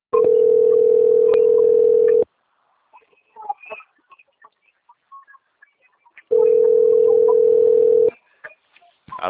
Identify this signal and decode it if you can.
Ringback tone (North America), 440 + 480 Hz.
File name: out- - - - - .87951